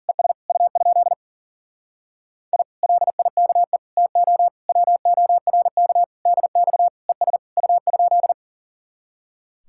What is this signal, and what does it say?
Morse code, heard as a keyed tone, about 52 wpm, 700 Hz. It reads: ES U? I LIKE TO WORK DX ES U?